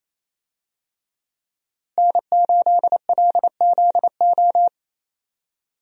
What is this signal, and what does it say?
Morse code, heard as a keyed tone, about 28 wpm, 700 Hz. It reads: N8LZO
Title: USA Call Signs - 5 Characters - Repeated 3x - 28wpm